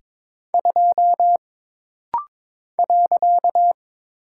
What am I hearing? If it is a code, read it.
Morse code, heard as a keyed tone, about 22 wpm, 700 Hz. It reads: 2 .
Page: This practice set is composed of Letters, Numbers, common punctuations, and common prosigns.